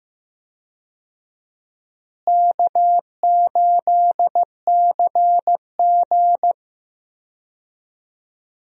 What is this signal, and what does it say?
Morse code, heard as a keyed tone, about 15 wpm, 700 Hz. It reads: K8CG
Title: USA Call Signs - 4 Characters - Repeated 3x - 15wpm